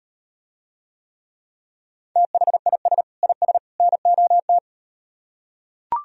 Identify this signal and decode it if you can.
Morse code, heard as a keyed tone, about 38 wpm, 700 Hz. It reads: THIS IS NOT